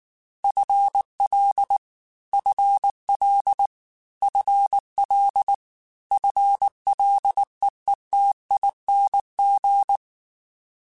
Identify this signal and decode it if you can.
Morse code, heard as a keyed tone, about 19 wpm, 800 Hz. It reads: FL FL FL FLEETING
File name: FL / FL mp3s / FL Prefixes 3x2 @18